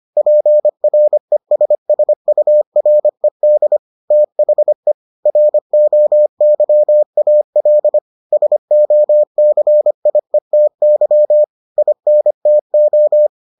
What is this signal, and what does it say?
Morse code, heard as a keyed tone, about 25 wpm, 600 Hz. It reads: PRESSURED THE ROYAL SOCIETY INTO